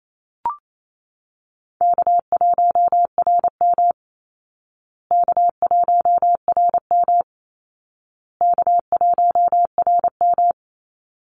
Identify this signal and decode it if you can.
Morse code, heard as a keyed tone, about 27 wpm, 700 Hz. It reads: K1RM K1RM K1RM
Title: Call Signs Repeated 2x - 28wpm